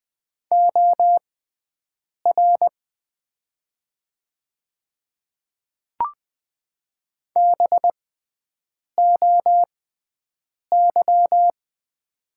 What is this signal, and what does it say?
Morse code, heard as a keyed tone, about 20 wpm, spaced as for 7 wpm, 700 Hz. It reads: OR BOY
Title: 6x Character Spacing - Top 200 Words - 20wpm